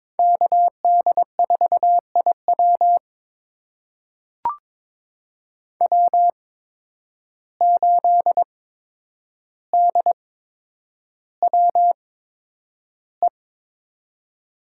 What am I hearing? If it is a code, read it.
Morse code, heard as a keyed tone, about 22 wpm, spaced as for 9 wpm, 700 Hz. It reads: KD4IW W8DWE
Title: USA Call Signs - 5 Characters - 8x Character Spacing - 22wpm